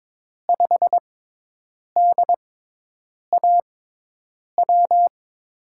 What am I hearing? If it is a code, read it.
Morse code, heard as a keyed tone, about 22 wpm, spaced as for 8 wpm, 700 Hz. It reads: 5DAW
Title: USA Call Signs - 5 Characters - 6x Character Spacing - 22wpm